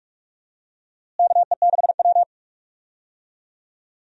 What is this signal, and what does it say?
Morse code, heard as a keyed tone, about 46 wpm, 700 Hz. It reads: KE6W
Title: USA Call Signs - 4 Characters - Repeated 2x - 45wpm